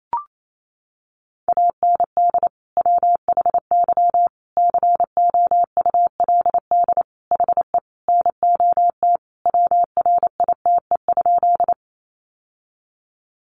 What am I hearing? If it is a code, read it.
Morse code, heard as a keyed tone, about 28 wpm, 700 Hz. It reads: AND WHY COULD HE NOT WRITE?
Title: Sentences from Top 600 Words 28wpm